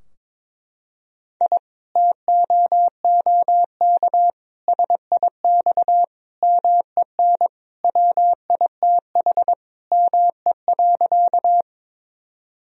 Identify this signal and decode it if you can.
Morse code, heard as a keyed tone, about 21 wpm, 700 Hz. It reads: I TOOK SIX MEN WITH ME.
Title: Sentences from Top 1000 Words - Review Part 2 - 22wpm